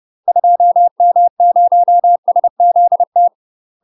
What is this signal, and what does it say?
Morse code, heard as a keyed tone, about 30 wpm, 700 Hz. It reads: 2M0SZT
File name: UK MASTER QSOs TRAINING / 30WPM / UK MASTER QSOs 0003